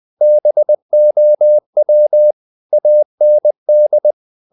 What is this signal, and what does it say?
Morse code, heard as a keyed tone, about 20 wpm, 600 Hz. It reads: BOW AND
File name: the old man and the sea part 2 20wpm